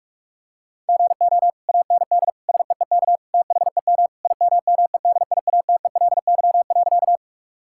Morse code, heard as a keyed tone, about 41 wpm, 700 Hz. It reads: GO AND SEEK THEM IMMEDIATELY.
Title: Sentences from Top 1600 Words 45wpm